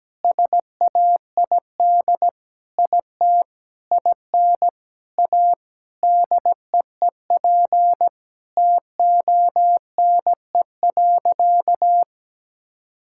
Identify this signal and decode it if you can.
Morse code, heard as a keyed tone, about 17 wpm, 700 Hz. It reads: SAID IT IN A DEEP TONE.
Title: Sentences from Top 1600 Words 17wpm